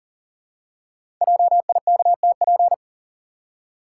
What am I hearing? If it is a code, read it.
Morse code, heard as a keyed tone, about 41 wpm, 700 Hz. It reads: JIKTP
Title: Code Groups - 5 Letters - 40wpm